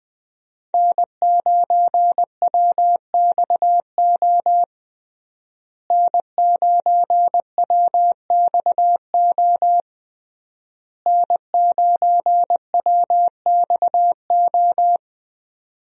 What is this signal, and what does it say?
Morse code, heard as a keyed tone, about 21 wpm, 700 Hz. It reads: N9WXO N9WXO N9WXO
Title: USA Call Signs - 5 Characters - Repeated 3x - 20wpm